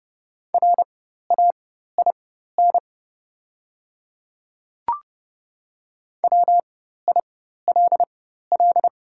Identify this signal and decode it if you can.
Morse code, heard as a keyed tone, about 30 wpm, spaced as for 14 wpm, 700 Hz. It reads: RAIN WILL
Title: 4x Character Spacing - Top 300 Words - 30wpm